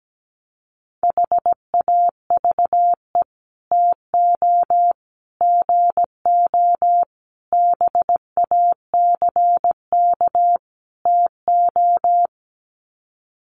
Morse code, heard as a keyed tone, about 17 wpm, 700 Hz. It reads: HAVE TO GO BACK TO
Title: Sets of 5 Words Encore2 17wpm